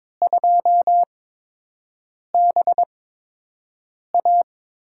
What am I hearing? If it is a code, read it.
Morse code, heard as a keyed tone, about 22 wpm, spaced as for 6 wpm, 700 Hz. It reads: 2BA
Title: USA Call Signs - 4 Characters - 8x Character Spacing - 22wpm